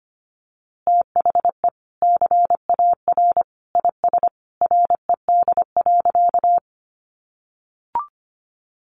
Morse code, heard as a keyed tone, about 25 wpm, 700 Hz. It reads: THE CAR IS RED.
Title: Easy Phrases 25wpm